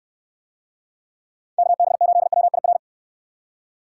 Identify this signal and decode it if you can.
Morse code, heard as a keyed tone, about 63 wpm, 700 Hz. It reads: KB8WIR